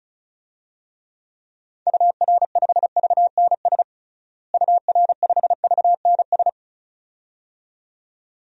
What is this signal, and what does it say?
Morse code, heard as a keyed tone, about 34 wpm, 700 Hz. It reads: UR5VNS UR5VNS
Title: DXCC-100 Call Signs - 6 Characters - Repeated 2x - 35wpm